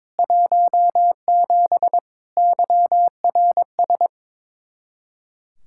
Morse code, heard as a keyed tone, about 22 wpm, 700 Hz. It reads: 17 YRS
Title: Speed-Racing Common QSO Phrases - 22wpm